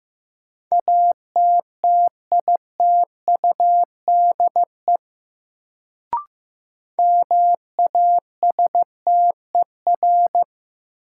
Morse code, heard as a keyed tone, about 15 wpm, 700 Hz. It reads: ATTITUDE MASTER
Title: Sentences from Top 1300 Words - New Word Drill Alt - 15wpm